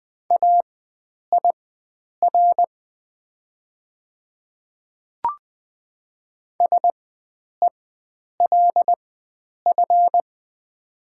Morse code, heard as a keyed tone, about 20 wpm, spaced as for 9 wpm, 700 Hz. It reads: AIR SELF
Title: 4x Character Spacing - Top 300 Words - 20wpm